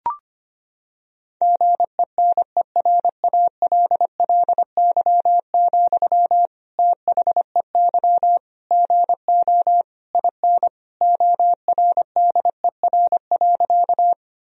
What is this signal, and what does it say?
Morse code, heard as a keyed tone, about 25 wpm, 700 Hz. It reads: GENERALLY, THEY GO IN ORDER.
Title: Sentences from Top 1000 Words - Review Part 4 - 25wpm